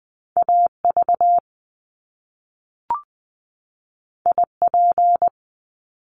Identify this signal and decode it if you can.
Morse code, heard as a keyed tone, about 20 wpm, 700 Hz. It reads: A V I P
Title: Top Two Letter Combinations in Words 20wpm